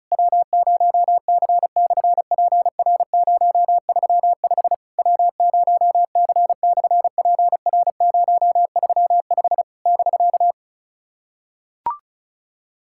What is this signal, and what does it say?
Morse code, heard as a keyed tone, about 33 wpm, 700 Hz. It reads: W0C/PR035 W0C/PR035 <BK>
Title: SOTA QSO Practice 35wpm